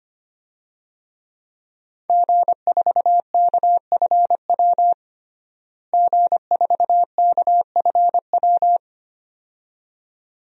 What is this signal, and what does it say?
Morse code, heard as a keyed tone, about 25 wpm, 700 Hz. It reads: G4KFW G4KFW
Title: DXCC-100 Call Signs - 5 Characters - Repeated 2x - 25wpm